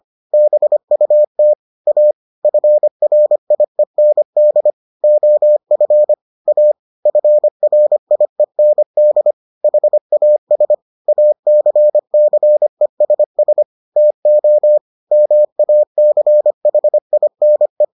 Morse code, heard as a keyed tone, about 25 wpm, 600 Hz. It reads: BUT A FRIEND OF A FRIEND HAS ACCESS TO MACHINE